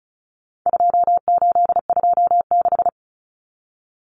Morse code, heard as a keyed tone, about 34 wpm, 700 Hz. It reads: 2826
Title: Numbers - 4-Digit 35wpm